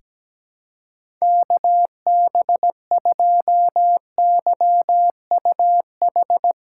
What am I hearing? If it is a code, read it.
Morse code, heard as a keyed tone, about 17 wpm, 700 Hz. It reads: KB2YUH
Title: USA Callsign DE Callsign - 6 Characters - 17wpm